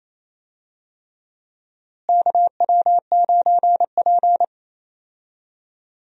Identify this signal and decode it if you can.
Morse code, heard as a keyed tone, about 28 wpm, 700 Hz. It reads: KW9P
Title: USA Call Signs - 4 Characters - Repeated 3x - 28wpm